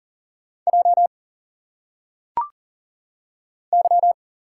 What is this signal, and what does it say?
Morse code, heard as a keyed tone, about 40 wpm, 700 Hz. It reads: J Y
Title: Single Letters - Phonetic Pronunciation 40wpm